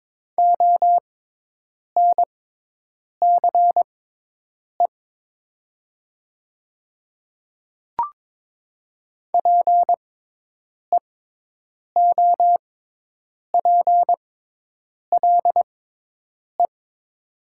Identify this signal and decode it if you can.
Morse code, heard as a keyed tone, about 22 wpm, spaced as for 8 wpm, 700 Hz. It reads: ONCE PEOPLE
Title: 6x Character Spacing - Top 300 Words - 22wpm